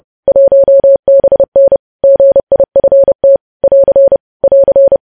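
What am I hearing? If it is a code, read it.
Morse code, heard as a keyed tone, about 30 wpm, 550 Hz. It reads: 1BN GIFT <AR> <AR>
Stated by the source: News headlines sent in Morse code at 30.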